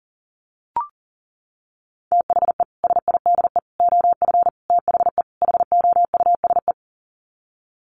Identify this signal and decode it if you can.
Morse code, heard as a keyed tone, about 40 wpm, 700 Hz. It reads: THE SIDE OF THE HOUSE
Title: Sets of 5 Words Encore2 40wpm